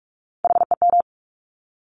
Morse code, heard as a keyed tone, about 46 wpm, 700 Hz. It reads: HEM
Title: Top Three Letter Combinations in Words 45wpm